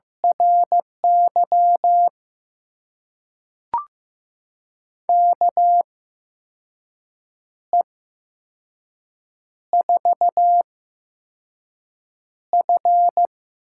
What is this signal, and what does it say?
Morse code, heard as a keyed tone, about 15 wpm, spaced as for 5 wpm, 700 Hz. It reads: RY KE4F
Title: USA Call Signs - 6 Characters - 8x Character Spacing - 15wpm